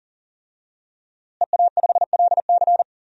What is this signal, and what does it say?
Morse code, heard as a keyed tone, about 37 wpm, 700 Hz. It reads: EA5LC
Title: DXCC-100 Call Signs - 5 Characters - Repeated 2x - 40wpm